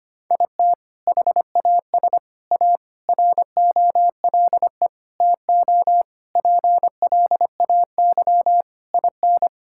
Morse code, heard as a keyed tone, about 25 wpm, 700 Hz. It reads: IT HAS A ROLE TO PLAY IN
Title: Sentences from Top 1000 Words - Review Part 6 - 25wpm